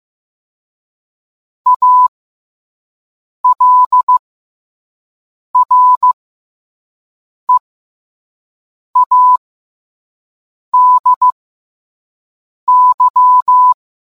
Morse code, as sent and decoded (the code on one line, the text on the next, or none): .- .-.. .-. .- -.. -.--
ALREADY